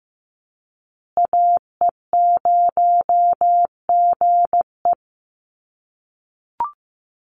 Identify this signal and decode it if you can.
Morse code, heard as a keyed tone, about 15 wpm, 700 Hz. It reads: AE0GE
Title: USA Call Signs Repeated 2x 15wpm